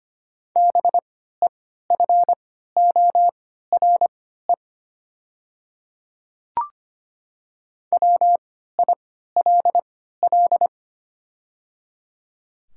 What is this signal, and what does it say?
Morse code, heard as a keyed tone, about 25 wpm, spaced as for 14 wpm, 700 Hz. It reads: BEFORE WILL
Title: 3x Character Spacing - Top 200 Words - 25wpm